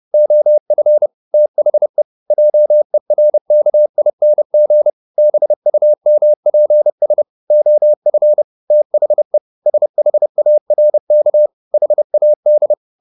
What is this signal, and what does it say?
Morse code, heard as a keyed tone, about 30 wpm, 600 Hz. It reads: OF THE JERKING BUMPS OF THE SHARK HAD